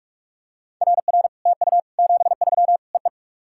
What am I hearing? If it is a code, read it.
Morse code, heard as a keyed tone, about 44 wpm, 700 Hz. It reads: RR TU 73 EE
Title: SOTA QSO Practice 45wpm